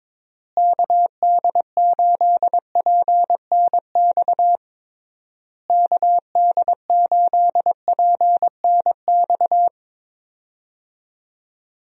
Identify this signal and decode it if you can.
Morse code, heard as a keyed tone, about 22 wpm, 700 Hz. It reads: KD8PNX KD8PNX
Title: USA Call Signs - 6 Characters - Repeated 3x - 22wpm